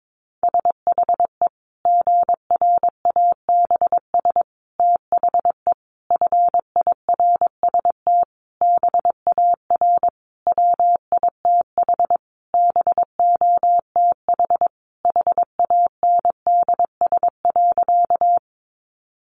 Morse code, heard as a keyed tone, about 22 wpm, 700 Hz. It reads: SHE GRABS THE FIRST BAR WITH BOTH HANDS.
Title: Easy Phrases 22wpm